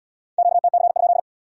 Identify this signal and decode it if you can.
Morse code, heard as a keyed tone, about 75 wpm, 700 Hz. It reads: QTC?